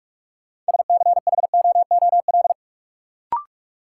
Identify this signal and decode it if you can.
Morse code, heard as a keyed tone, about 46 wpm, 700 Hz. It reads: SCHOOL
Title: Sentences from Top 300 Words - New Word Drill - 45wpm